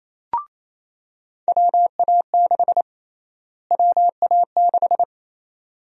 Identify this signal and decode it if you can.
Morse code, heard as a keyed tone, about 28 wpm, 700 Hz. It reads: WA6 WA6
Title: USA Call Sign Prefixes - Repeated 2x - 28wpm